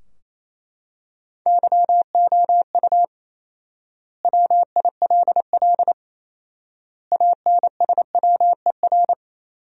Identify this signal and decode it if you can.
Morse code, heard as a keyed tone, about 28 wpm, 700 Hz. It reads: YOU WILL ANSWER
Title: 4x Word-Spacing - Sentences from Top 300 Words - 28wpm